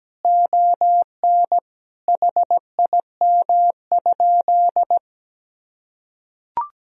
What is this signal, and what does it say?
Morse code, heard as a keyed tone, about 17 wpm, 700 Hz. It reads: ON HIM?
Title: Sentences from Top 1000 Words - Review Part 1 - 17wpm